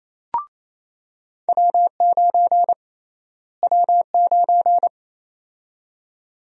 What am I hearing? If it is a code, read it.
Morse code, heard as a keyed tone, about 27 wpm, 700 Hz. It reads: W9 W9
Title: USA Call Sign Prefixes - Repeated 2x - 28wpm